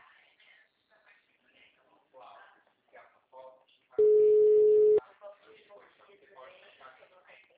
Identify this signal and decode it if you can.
Ringback tone (Europe, 425 Hz), with speech.
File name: out- -2010- - - .169